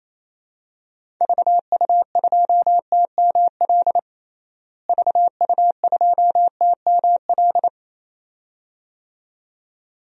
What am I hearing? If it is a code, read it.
Morse code, heard as a keyed tone, about 27 wpm, 700 Hz. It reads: VU2TML VU2TML
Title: DXCC-100 Call Signs - 6 Characters - Repeated 2x - 28wpm